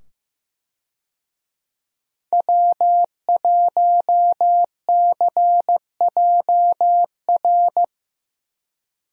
Morse code, heard as a keyed tone, about 15 wpm, 700 Hz. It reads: W1CJR
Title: USA Call Signs - 5 Characters - 15wpm